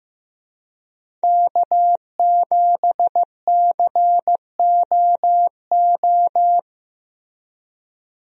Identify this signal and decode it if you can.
Morse code, heard as a keyed tone, about 15 wpm, 700 Hz. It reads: K7COO
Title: USA Call Signs - 5 Characters - Repeated 3x - 15wpm